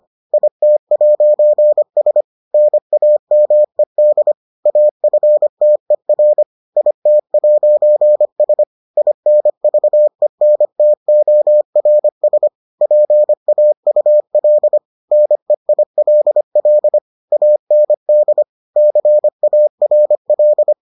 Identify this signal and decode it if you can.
Morse code, heard as a keyed tone, about 25 wpm, 600 Hz. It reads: IT'S NAMED AFTER IT'S INVENTORS PAUL NEILL AND CARL